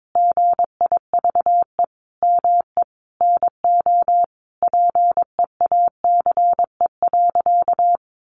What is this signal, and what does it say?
Morse code, heard as a keyed tone, about 22 wpm, 700 Hz. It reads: GIVE ME NO PEACE.
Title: Sentences from Top 1500 Words 22wpm